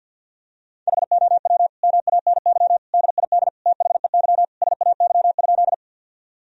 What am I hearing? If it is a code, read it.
Morse code, heard as a keyed tone, about 51 wpm, 700 Hz. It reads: HOW MANY DID THEY SAY?
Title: Sentences from Top 1000 Words - Review Part 1 - 50wpm